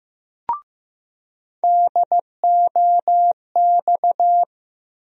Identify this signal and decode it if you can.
Morse code, heard as a keyed tone, about 15 wpm, 700 Hz. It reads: DOX
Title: USA Call Sign Postfixes - 3 Letters 15wpm